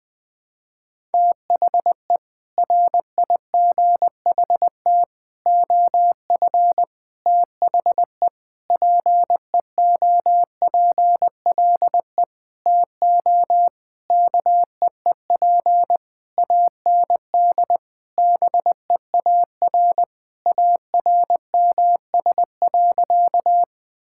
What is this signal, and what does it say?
Morse code, heard as a keyed tone, about 20 wpm, 700 Hz. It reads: THE RIGHT OF THE PEOPLE TO KEEP AND BEAR ARMS.
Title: Sentences with 8-Word Phrases - 20wpm